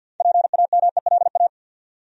Morse code, heard as a keyed tone, about 53 wpm, 700 Hz. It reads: PAMELA
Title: USA Common Names - 50wpm